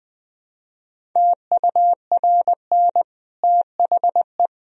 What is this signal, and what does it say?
Morse code, heard as a keyed tone, about 20 wpm, 700 Hz. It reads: TURN THE
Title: Sentences from Top 1000 Words - Review Part 1 - 20wpm